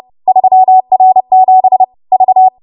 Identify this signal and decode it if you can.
Morse code, heard as a keyed tone, about 30 wpm, 750 Hz. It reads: 3R7 V